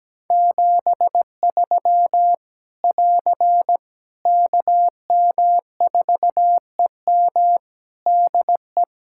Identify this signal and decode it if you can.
Morse code, heard as a keyed tone, about 17 wpm, 700 Hz. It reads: 73 <AR> KM4EM DE
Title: Practice QSO - Exchange 4 - Ending - 17wpm